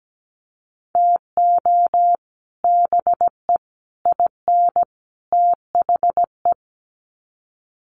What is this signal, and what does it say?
Morse code, heard as a keyed tone, about 17 wpm, 700 Hz. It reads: TO BE IN THE
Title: Sets of 4 Words 17wpm